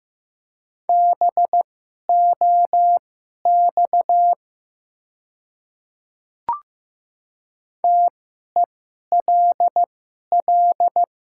Morse code, heard as a keyed tone, about 15 wpm, spaced as for 11 wpm, 700 Hz. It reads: BOX TELL
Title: 2x Character Spacing - Top 500 Words - 15wpm